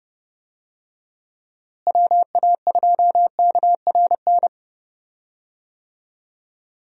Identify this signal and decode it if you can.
Morse code, heard as a keyed tone, about 30 wpm, 700 Hz. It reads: WA2KRN
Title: USA Call Signs - 6 Characters - Repeated 2x - 30wpm